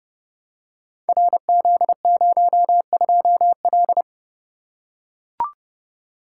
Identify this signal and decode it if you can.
Morse code, heard as a keyed tone, about 30 wpm, 700 Hz. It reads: RZ02L
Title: Code Groups - 5 Letter:Numbers - 30wpm